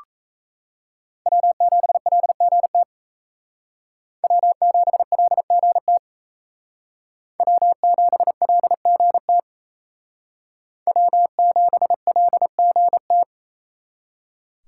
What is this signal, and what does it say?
Morse code, heard as a keyed tone, about 35 wpm, 700 Hz. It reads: W7LGT W7LGT W7LGT W7LGT